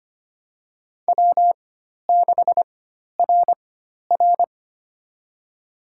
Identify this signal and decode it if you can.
Morse code, heard as a keyed tone, about 25 wpm, spaced as for 12 wpm, 700 Hz. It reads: W6RR
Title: USA Call Signs - 4 Characters - 4x Character Spacing - 25wpm